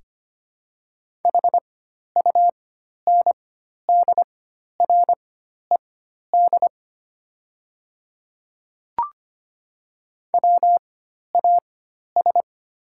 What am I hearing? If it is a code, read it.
Morse code, heard as a keyed tone, about 25 wpm, spaced as for 12 wpm, 700 Hz. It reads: HUNDRED WAS
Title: 4x Character Spacing - Top 500 Words - 25wpm